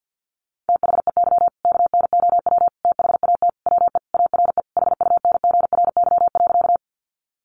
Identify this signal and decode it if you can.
Morse code, heard as a keyed tone, about 50 wpm, 700 Hz. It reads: THEY KNOW THAT WE ARE HUNGRY.
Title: Sentences from Top 1400 Words 50wpm